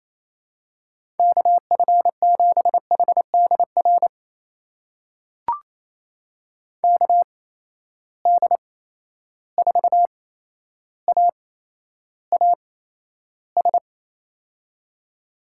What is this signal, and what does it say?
Morse code, heard as a keyed tone, about 28 wpm, spaced as for 12 wpm, 700 Hz. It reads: KF7HDR KD4AAS
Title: USA Call Signs - 6 Characters - 8x Character Spacing - 28wpm